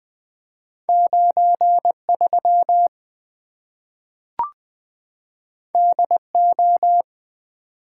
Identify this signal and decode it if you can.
Morse code, heard as a keyed tone, about 20 wpm, 700 Hz. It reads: 9 3 D O
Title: Two Letter-Number Combinations - 20wpm